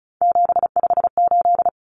Morse code, heard as a keyed tone, about 34 wpm, 700 Hz. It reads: 758
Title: Numbers - 3-Digit 35wpm